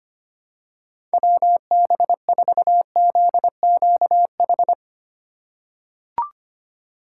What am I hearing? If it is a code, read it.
Morse code, heard as a keyed tone, about 25 wpm, 700 Hz. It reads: WB4ZQH
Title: USA Call Signs - 6 Characters - 4x Character Spacing - 25wpm